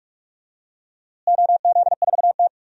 Morse code, heard as a keyed tone, about 46 wpm, 700 Hz. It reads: OZ4T
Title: DXCC-100 Call Signs - 4 Characters - Repeated 2x - 45wpm